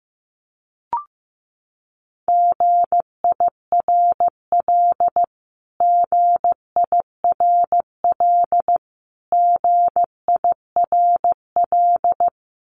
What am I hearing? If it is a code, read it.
Morse code, heard as a keyed tone, about 15 wpm, 700 Hz. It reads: GIRL GIRL GIRL
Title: Top 500 Words Repeated 15wpm